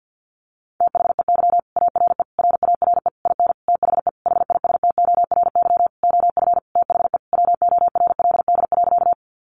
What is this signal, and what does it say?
Morse code, heard as a keyed tone, about 50 wpm, 700 Hz. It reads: THEY ARE RARE IN THE HISTORY OF THE WORLD.
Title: Sentences from Top 1100 Words 50wpm